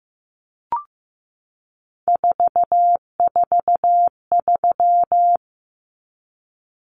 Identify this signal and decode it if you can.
Morse code, heard as a keyed tone, about 15 wpm, 700 Hz. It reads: 443
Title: Numbers - 3-Digit 15wpm